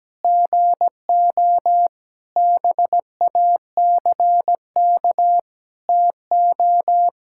Morse code, heard as a keyed tone, about 17 wpm, 700 Hz. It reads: GO BACK TO
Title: Pangram Challenge v2 - 17wpm